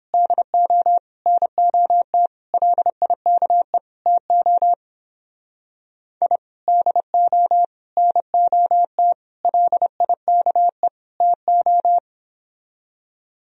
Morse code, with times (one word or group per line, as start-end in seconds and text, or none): 0.1-1.0 DO
1.3-2.3 NOT
2.5-3.8 LIKE
4.1-4.7 TO
6.2-6.4 I
6.7-7.6 DO
8.0-9.1 NOT
9.4-10.9 LIKE
11.2-12.0 TO